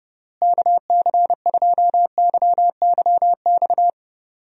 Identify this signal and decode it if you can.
Morse code, heard as a keyed tone, about 30 wpm, 700 Hz. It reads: KC2YYX
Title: USA Call Signs - 6 Characters - Repeated 2x - 30wpm